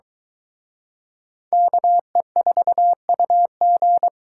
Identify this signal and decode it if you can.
Morse code, heard as a keyed tone, about 22 wpm, 700 Hz. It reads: KE4UG